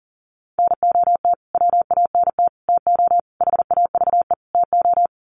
Morse code, heard as a keyed tone, about 40 wpm, 700 Hz. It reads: NOT WANT TO HAVE TO
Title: Sets of 5 Words Encore2 40wpm